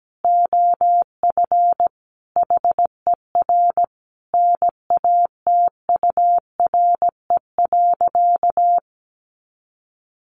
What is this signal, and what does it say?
Morse code, heard as a keyed tone, about 17 wpm, 700 Hz. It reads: OF HER NATURE.
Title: Sentences from Top 1300 Words 17wpm